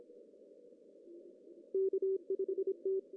Morse code, heard as a keyed tone, about 26 wpm, 380 Hz. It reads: K5T
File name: CW / cqww-cw-k5tr